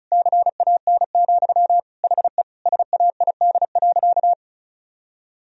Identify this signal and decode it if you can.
Morse code, heard as a keyed tone, about 36 wpm, 700 Hz. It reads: CAN, HE SAID.
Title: Sentences from Top 1000 Words - Review Part 1 - 35wpm